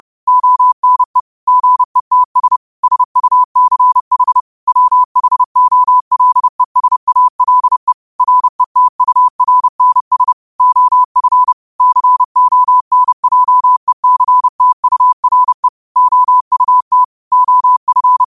Morse code, heard as a keyed tone, about 30 wpm, 1000 Hz. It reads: ONE GETS SUCH WHOLESALE RETURNS OF CONJECTURE OUT OF